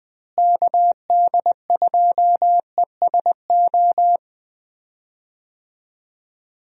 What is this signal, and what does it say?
Morse code, heard as a keyed tone, about 21 wpm, 700 Hz. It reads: KD2ESO
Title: USA Call Signs - 6 Characters - Repeated 3x - 20wpm